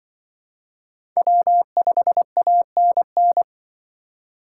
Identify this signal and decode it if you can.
Morse code, heard as a keyed tone, about 23 wpm, 700 Hz. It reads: W5ANN
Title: Speed Racing - USA 1x3 Call Signs 20wpm